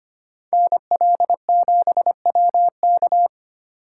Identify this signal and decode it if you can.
Morse code, heard as a keyed tone, about 32 wpm, 700 Hz. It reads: NL7WK